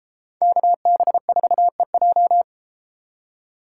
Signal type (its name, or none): Morse code